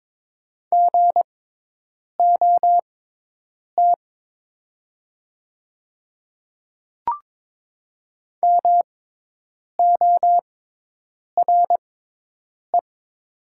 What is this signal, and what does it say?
Morse code, heard as a keyed tone, about 22 wpm, spaced as for 8 wpm, 700 Hz. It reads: GOT MORE
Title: 6x Character Spacing - Top 300 Words - 22wpm